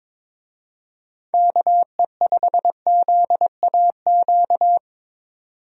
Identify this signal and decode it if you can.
Morse code, heard as a keyed tone, about 22 wpm, 700 Hz. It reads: KE5ZAQ
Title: USA Call Signs - 6 Characters - 22wpm